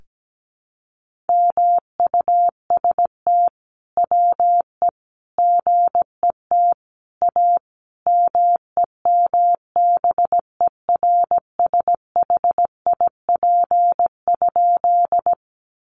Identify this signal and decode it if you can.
Morse code, heard as a keyed tone, about 17 wpm, 700 Hz. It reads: MUST WE GET A MEMBERSHIP?
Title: Sentences from Top 1400 Words 17wpm